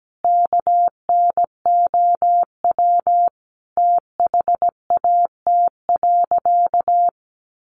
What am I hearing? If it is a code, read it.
Morse code, heard as a keyed tone, about 17 wpm, 700 Hz. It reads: KNOW THAT.
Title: Sentences from Top 500 Words 17wpm